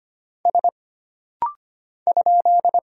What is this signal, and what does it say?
Morse code, heard as a keyed tone, about 25 wpm, 700 Hz. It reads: S ?
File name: Single Character (Letter:Number:Punctuation:Prosign) - Rapid-Fire - 25wpm